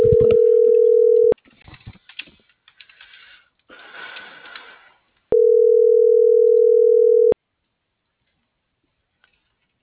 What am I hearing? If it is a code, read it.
Ringback tone (North America), 440 + 480 Hz.